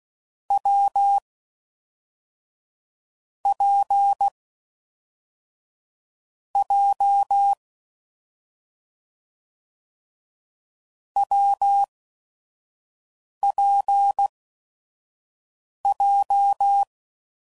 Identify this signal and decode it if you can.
Morse code, heard as a keyed tone, about 16 wpm, spaced as for 5 wpm, 800 Hz. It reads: W P J WPJ
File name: Combinations / Difficult Combos / AWJ1